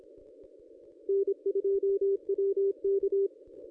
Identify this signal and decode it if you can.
Morse code, heard as a keyed tone, about 26 wpm, 390 Hz. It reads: N2WK
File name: contests / ss-cw-k5tr-op-n5rz -1100